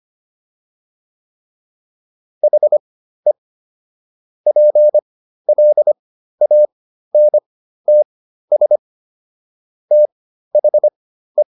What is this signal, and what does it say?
Morse code, heard as a keyed tone, about 25 wpm, spaced as for 13 wpm, 600 Hz. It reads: HE PLANTS THE